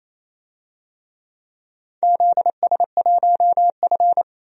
Morse code, heard as a keyed tone, about 28 wpm, 700 Hz. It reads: ZS1F
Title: DXCC-100 Call Signs - 4 Characters - 28wpm